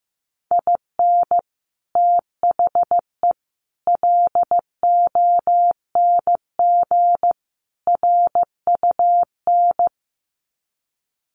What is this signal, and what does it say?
Morse code, heard as a keyed tone, about 15 wpm, 700 Hz. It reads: IN THE LONG RUN
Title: Sets of 4 Words 15wpm